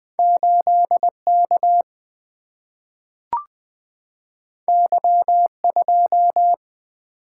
Morse code, heard as a keyed tone, about 20 wpm, 700 Hz. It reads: 8 K Y 2
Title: Single-Letter Single-Number 20wpm